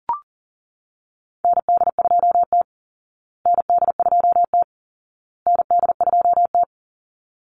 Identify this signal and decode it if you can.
Morse code, heard as a keyed tone, about 40 wpm, 700 Hz. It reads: ND2T ND2T ND2T
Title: Call Signs Repeated 2x - 40wpm